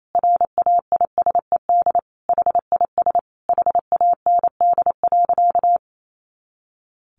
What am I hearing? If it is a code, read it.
Morse code, heard as a keyed tone, about 28 wpm, 700 Hz. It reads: RAISED HIS HAND.
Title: A Sound Of Thunder 28wpm